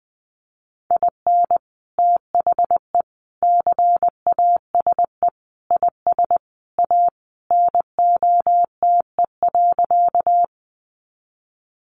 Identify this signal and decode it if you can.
Morse code, heard as a keyed tone, about 20 wpm, 700 Hz. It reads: IN THE CASE IS A NOTE.
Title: Sentences from Top 700 Words 20wpm